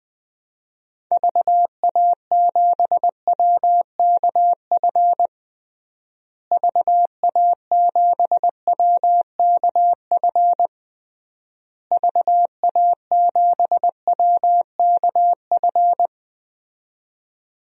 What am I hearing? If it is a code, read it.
Morse code, heard as a keyed tone, about 21 wpm, 700 Hz. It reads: VA7WKF VA7WKF VA7WKF
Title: DXCC-100 Call Signs - 6 Characters - Repeated 3x - 20wpm